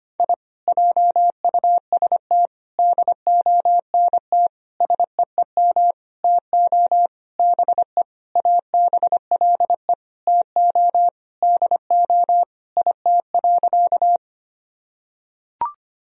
Morse code, heard as a keyed tone, about 25 wpm, 700 Hz. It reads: I JUST DONT SEEM TO BE ABLE TO DO IT.
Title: Sentences from Top 1000 Words - Review Part 2 - 25wpm